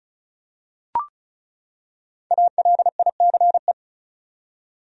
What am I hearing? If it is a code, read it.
Morse code, heard as a keyed tone, about 34 wpm, 700 Hz. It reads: ALICE
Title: USA Common Names - Max 5 Letters - 35wpm